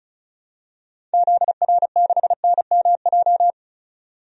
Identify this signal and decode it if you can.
Morse code, heard as a keyed tone, about 34 wpm, 700 Hz. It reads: ZR6NMJ
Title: DXCC-100 Call Signs - 6 Characters - 35wpm